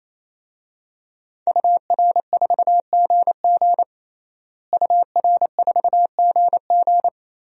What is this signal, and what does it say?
Morse code, heard as a keyed tone, about 28 wpm, 700 Hz. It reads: UR4GG UR4GG
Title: DXCC-100 Call Signs - 5 Characters - Repeated 3x - 28wpm